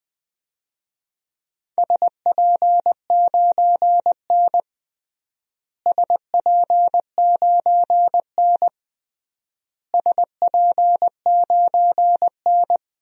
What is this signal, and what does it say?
Morse code, heard as a keyed tone, about 20 wpm, 700 Hz. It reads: SP9N SP9N SP9N
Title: DXCC-100 Call Signs - 4 Characters - Repeated 3x - 20wpm